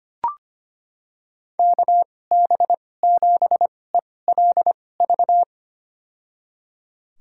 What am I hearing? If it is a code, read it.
Morse code, heard as a keyed tone, about 25 wpm, spaced as for 18 wpm, 700 Hz. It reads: KB7ELV
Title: USA Call Signs - 6 Characters - 2x Character Spacing - 25wpm